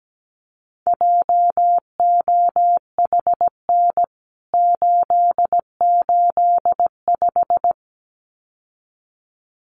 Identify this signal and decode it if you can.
Morse code, heard as a keyed tone, about 17 wpm, 700 Hz. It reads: JOHN 885
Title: CWT Exchange 17wpm